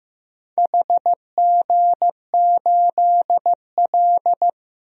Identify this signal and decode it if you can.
Morse code, heard as a keyed tone, about 15 wpm, 700 Hz. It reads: HG8L
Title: DXCC-100 Call Signs - 4 Characters - Repeated 3x - 15wpm